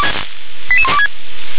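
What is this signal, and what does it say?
Five-tone selective call (ZVEI), heard as quick runs of five tones, tones of about 70 ms. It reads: 88126
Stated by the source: Five-tone ZVEI selective call received by SDR.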